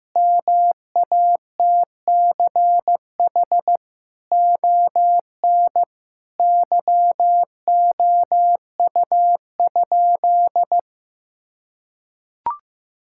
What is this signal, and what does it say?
Morse code, heard as a keyed tone, about 15 wpm, 700 Hz. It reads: MATCH ON YOU?
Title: Sentences from Top 1600 Words 15wpm